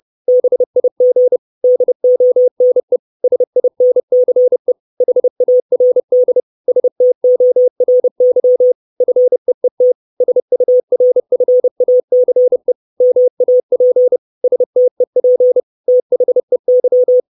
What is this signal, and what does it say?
Morse code, heard as a keyed tone, about 30 wpm, 500 Hz. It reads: BIG DONE SINCE HARD STORY FEET SURFACE MAP STEP THEY
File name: 632 Words 30WPM